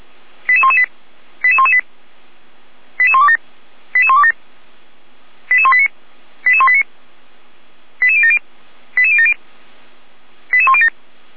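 Five-tone selective call (ZVEI), heard as quick runs of five tones, tones of about 70 ms. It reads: 88118 88118 88127 88127 88189 88189 88070 88070 88107
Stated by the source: ZVEI five-tone call received by SDR, logged on the Feuerwehr Orlamünde (fire brigade) server.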